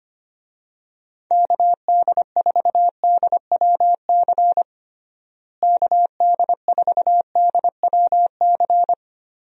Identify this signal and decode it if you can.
Morse code, heard as a keyed tone, about 25 wpm, 700 Hz. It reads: KD4DWC KD4DWC
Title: USA Call Signs - 6 Characters - Repeated 3x - 25wpm